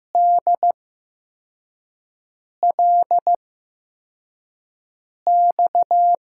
Morse code, heard as a keyed tone, about 15 wpm, spaced as for 4 wpm, 700 Hz. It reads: DLX
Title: USA Call Signs - 6 Characters - 8x Character Spacing - 15wpm